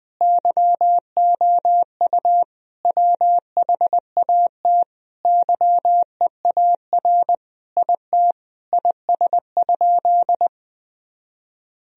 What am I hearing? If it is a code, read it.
Morse code, heard as a keyed tone, about 20 wpm, 700 Hz. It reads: YOU WHAT YEAR IT IS?
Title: Sentences from Top 1000 Words - Review Part 1 - 20wpm